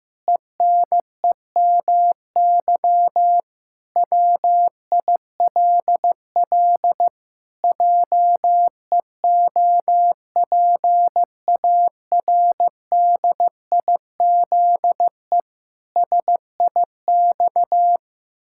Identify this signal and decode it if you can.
Morse code, heard as a keyed tone, about 15 wpm, 700 Hz. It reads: ENEMY WILL JEOPARDIZE SIX
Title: Pangram Challenge v2 - 15wpm